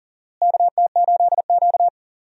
Morse code, heard as a keyed tone, about 40 wpm, 700 Hz. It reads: KT8Q
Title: USA Call Signs - 4 Characters - Repeated 2x - 40wpm